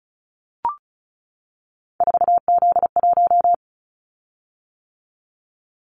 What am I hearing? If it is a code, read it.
Morse code, heard as a keyed tone, about 36 wpm, 700 Hz. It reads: 4Z1
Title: Callsign Prefixes for Top 50 DXCC - 35wpm